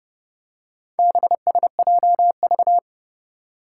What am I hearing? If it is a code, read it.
Morse code, heard as a keyed tone, about 30 wpm, 700 Hz. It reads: BSJV
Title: Four Challenging Letter-Number Combinations 30wpm